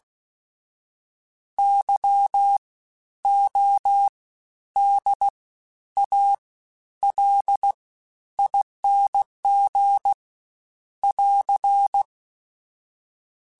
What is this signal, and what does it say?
Morse code, heard as a keyed tone, about 16 wpm, 800 Hz. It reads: Y O D A L ING <AR>
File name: Special INGs-mp3s / Special INGs-mp3s / SlowToFaster ing @15